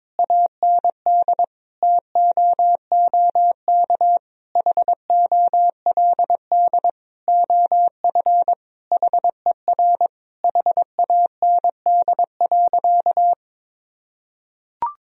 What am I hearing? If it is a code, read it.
Morse code, heard as a keyed tone, about 22 wpm, 700 Hz. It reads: AND TOOK HOLD OF HER HAND.
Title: Sentences from Top 1000 Words - Review Part 2 - 22wpm